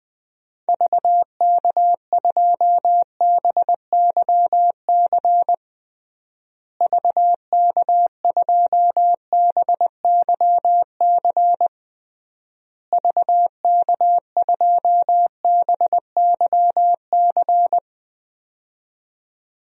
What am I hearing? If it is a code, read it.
Morse code, heard as a keyed tone, about 20 wpm, 700 Hz. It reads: VK2BYC VK2BYC VK2BYC
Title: DXCC-100 Call Signs - 6 Characters - Repeated 3x - 20wpm